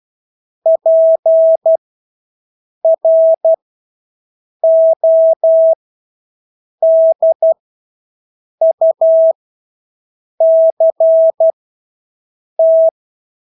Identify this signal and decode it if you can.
Morse code, heard as a keyed tone, about 12 wpm, spaced as for 6 wpm, 650 Hz. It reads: PRODUCT
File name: Engelske ord 12-6 wpm